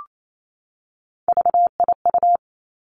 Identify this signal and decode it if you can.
Morse code, heard as a keyed tone, about 28 wpm, 700 Hz. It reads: VIU
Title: Three Challenging Letter-Number Combinations 28wpm